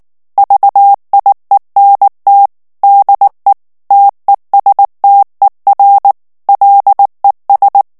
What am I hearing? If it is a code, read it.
Morse code, heard as a keyed tone, about 19 wpm, 800 Hz. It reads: VIENT DE TESTER LES
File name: morse mp3 / FRANCAIS TEXTE CLAIR 1 VITESSE 18 MOTS MINUTES